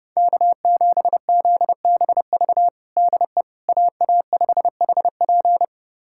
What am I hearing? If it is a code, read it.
Morse code, heard as a keyed tone, about 30 wpm, 700 Hz. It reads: K7ZBV DE AA5HP
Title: USA Callsign DE Callsign - 5 Characters - 30wpm